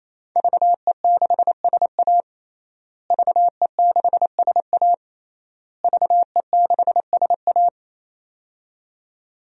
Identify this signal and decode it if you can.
Morse code, heard as a keyed tone, about 27 wpm, 700 Hz. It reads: VE6SA VE6SA VE6SA
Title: DXCC-100 Call Signs - 5 Characters - Repeated 3x - 28wpm